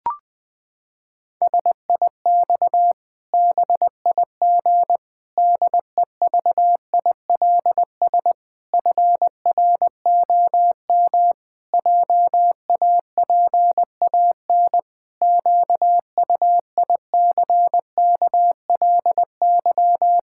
Morse code, heard as a keyed tone, about 20 wpm, 700 Hz. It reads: SIX BIG DEVILS FROM JAPAN QUICKLY
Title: Pangram Challenge v2 - 20wpm